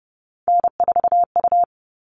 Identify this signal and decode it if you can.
Morse code, heard as a keyed tone, about 30 wpm, 700 Hz. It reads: N4U
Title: All Cut Numbers - 3-Digit 30wpm